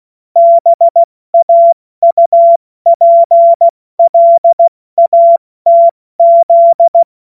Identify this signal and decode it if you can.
Morse code, heard as a keyed tone, about 16 wpm, spaced as for 14 wpm, 680 Hz. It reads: BAUPLATZ
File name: Morsekurs / Witze / Witz 30 Januar 2024 70 F80